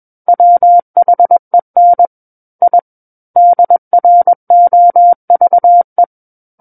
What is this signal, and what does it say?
Morse code, heard as a keyed tone, about 21 wpm, 700 Hz. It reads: WHEN I DROVE
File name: Book One / 12-Chapter 12 - 21 WPM